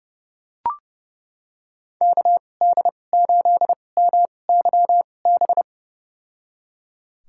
Morse code, heard as a keyed tone, about 30 wpm, spaced as for 22 wpm, 700 Hz. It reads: KD8MYB
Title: USA Call Signs - 2x Character Spacing - 30wpm